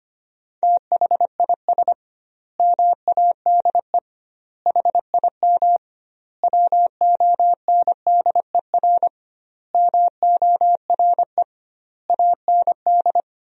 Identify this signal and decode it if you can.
Morse code, heard as a keyed tone, about 25 wpm, 700 Hz. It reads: THIS MADE HIM WONDER MORE AND
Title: 2x Word-Spacing - Sentences from Top 500 Words - 25wpm